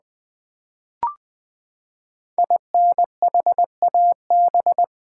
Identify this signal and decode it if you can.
Morse code, heard as a keyed tone, about 20 wpm, 700 Hz. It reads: INHAB
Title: Build A Word - 5 Letters At A Time - 10-Letter Words - 20wpm